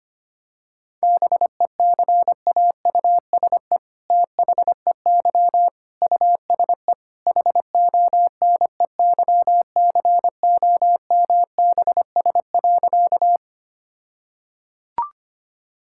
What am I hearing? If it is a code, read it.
Morse code, heard as a keyed tone, about 25 wpm, 700 Hz. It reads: BECAUSE THEY USE HONEYCOMBS.
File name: practice / 25wpm / All-Age Appropriate Jokes - 25wpm